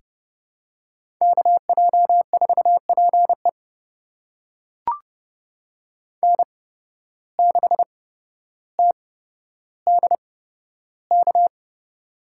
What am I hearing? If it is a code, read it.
Morse code, heard as a keyed tone, about 30 wpm, spaced as for 13 wpm, 700 Hz. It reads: KJ4PE N6TDK
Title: USA Call Signs - 5 Characters - 8x Character Spacing - 30wpm